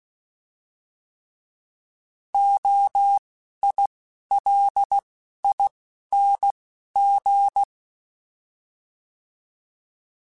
Morse code, heard as a keyed tone, about 16 wpm, spaced as for 11 wpm, 800 Hz. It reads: OILING
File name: Alphabetical / Alphabetical mp3s / Os @10